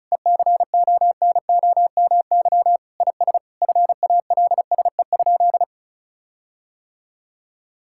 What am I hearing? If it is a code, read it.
Morse code, heard as a keyed tone, about 36 wpm, 700 Hz. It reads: ECONOMY IS FALSE?
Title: Sentences from Top 1600 Words 35wpm